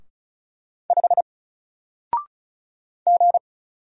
Morse code, heard as a keyed tone, about 36 wpm, 700 Hz. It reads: 5 G
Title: Single Character (Letter:Number:Punctuation:Prosign) Alt 35wpm